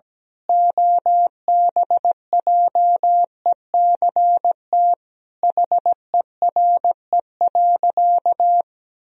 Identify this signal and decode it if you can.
Morse code, heard as a keyed tone, about 17 wpm, 700 Hz. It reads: OBJECT HERE.
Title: Sentences from Top 1000 Words - Review Part 3 - 17wpm